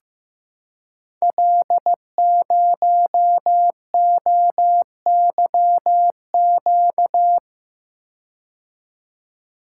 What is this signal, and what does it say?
Morse code, heard as a keyed tone, about 15 wpm, 700 Hz. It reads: L0OYQ
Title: Code Groups - 5 Letter:Numbers - 15wpm